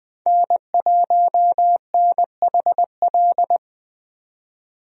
Morse code, heard as a keyed tone, about 21 wpm, 700 Hz. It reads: N1NHL
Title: USA Call Signs - 5 Characters - Repeated 2x - 20wpm